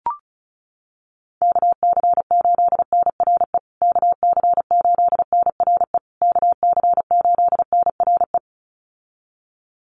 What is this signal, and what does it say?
Morse code, heard as a keyed tone, about 34 wpm, 700 Hz. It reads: KC8NRE KC8NRE KC8NRE
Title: USA Call Signs Repeated 3x 35wpm